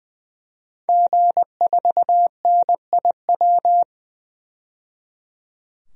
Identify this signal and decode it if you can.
Morse code, heard as a keyed tone, about 21 wpm, 700 Hz. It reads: G4NIW
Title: DXCC-100 Call Signs - 5 Characters - 20wpm